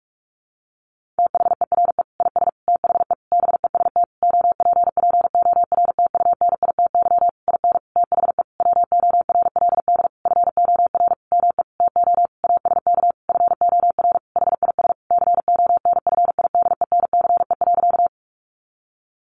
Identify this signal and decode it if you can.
Morse code, heard as a keyed tone, about 44 wpm, 700 Hz. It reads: THERE IS THE BEST OPPORTUNITY IN THE WORLD FOR ME TO ASK FOR HIS CONFIDENCE.
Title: Sentences from Top 1500 Words 45wpm